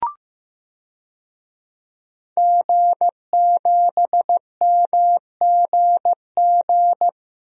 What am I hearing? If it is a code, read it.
Morse code, heard as a keyed tone, about 16 wpm, 700 Hz. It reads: G7MGG
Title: DXCC-100 Call Signs - 5 Characters - Repeated 2x - 15wpm